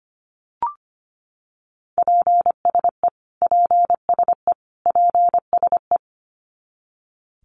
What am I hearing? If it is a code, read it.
Morse code, heard as a keyed tone, about 25 wpm, 700 Hz. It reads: PSE PSE PSE
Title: Abbreviations Repeated 3x - 25wpm